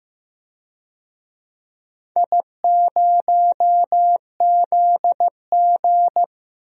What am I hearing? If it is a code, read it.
Morse code, heard as a keyed tone, about 15 wpm, 700 Hz. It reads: I0ZG
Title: DXCC-100 Call Signs - 4 Characters - 15wpm